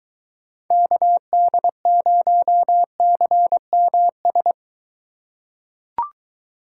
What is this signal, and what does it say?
Morse code, heard as a keyed tone, about 22 wpm, 700 Hz. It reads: KD0CMS